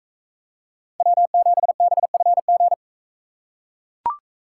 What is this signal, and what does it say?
Morse code, heard as a keyed tone, about 41 wpm, 700 Hz. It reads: W7BFG